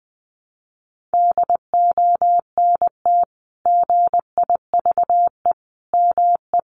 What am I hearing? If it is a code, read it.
Morse code, heard as a keyed tone, about 20 wpm, 700 Hz. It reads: DONT GIVE ME
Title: Sentences from Top 600 Words 20wpm